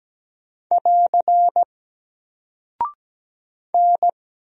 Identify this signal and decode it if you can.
Morse code, heard as a keyed tone, about 17 wpm, 700 Hz. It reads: <AR> N
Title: Single Character (Letter:Number:Punctuation:Prosign) Alt 17wpm